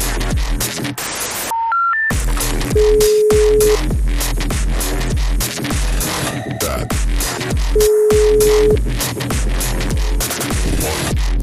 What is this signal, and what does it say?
Ringback tone (Europe, 425 Hz), with noise.